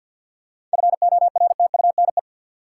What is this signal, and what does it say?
Morse code, heard as a keyed tone, about 50 wpm, 700 Hz. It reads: FORTUNE
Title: Seven Letter Words - 50wpm